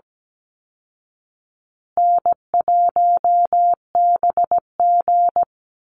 Morse code, heard as a keyed tone, about 17 wpm, 700 Hz. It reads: N1BG
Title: Easy Call Signs Repeated 2x - 17wpm